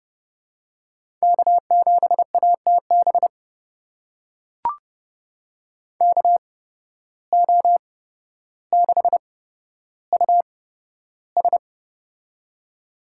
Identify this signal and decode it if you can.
Morse code, heard as a keyed tone, about 30 wpm, spaced as for 13 wpm, 700 Hz. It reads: K7ATB KO6US
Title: USA Call Signs - 5 Characters - 8x Character Spacing - 30wpm